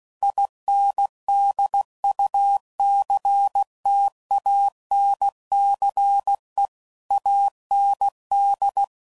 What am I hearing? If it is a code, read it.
Morse code, heard as a keyed tone, about 16 wpm, 800 Hz. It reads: INDUCTANCE AND
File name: CW Code Groups @15WPM / Technical Subjects / Reactance